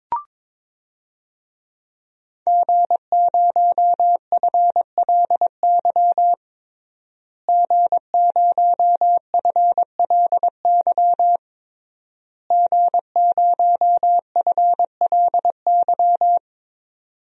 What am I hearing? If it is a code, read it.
Morse code, heard as a keyed tone, about 22 wpm, 700 Hz. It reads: G0FLY G0FLY G0FLY
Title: DXCC-100 Call Signs - 5 Characters - Repeated 3x - 22wpm